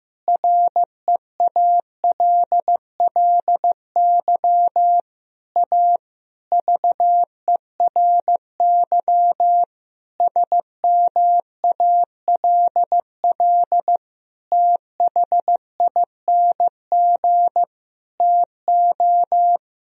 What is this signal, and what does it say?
Morse code, heard as a keyed tone, about 15 wpm, 700 Hz. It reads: REALLY A VERY SMALL THING TO
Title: Sentences from Top 1000 Words - Review Part 1 - 15wpm